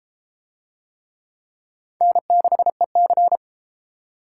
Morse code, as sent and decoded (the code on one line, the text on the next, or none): -. -.... -.-.
N6EC